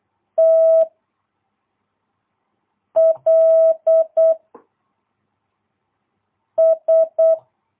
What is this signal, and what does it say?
Morse code, heard as a keyed tone, about 8 wpm, 660 Hz. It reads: T L S
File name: jh8cbh / 2022cw